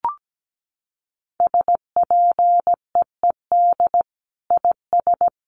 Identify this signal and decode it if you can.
Morse code, heard as a keyed tone, about 17 wpm, 700 Hz. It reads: SPEED IS
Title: Sentences from Top 1300 Words 17wpm